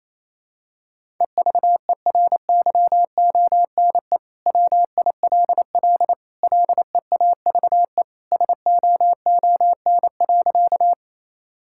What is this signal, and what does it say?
Morse code, heard as a keyed tone, about 28 wpm, 700 Hz. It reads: EVERYONE WILL LEAVE SOON.
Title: Sentences from Top 1000 Words - Review Part 5 - 28wpm